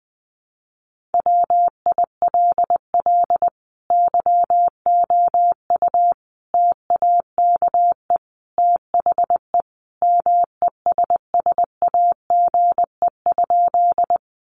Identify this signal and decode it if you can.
Morse code, heard as a keyed tone, about 20 wpm, 700 Hz. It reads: WILL YOU TAKE THE MESSAGE?
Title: Sentences from Top 1100 Words 20wpm